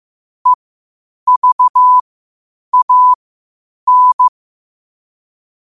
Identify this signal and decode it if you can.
Morse code, heard as a keyed tone, about 15 wpm, spaced as for 8 wpm, 1000 Hz. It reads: EVAN
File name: qotdfiles / 8wpm